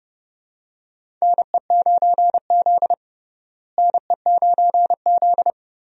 Morse code, heard as a keyed tone, about 30 wpm, 700 Hz. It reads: NE9Z NE9Z
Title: USA Call Signs - 4 Characters - Repeated 2x - 30wpm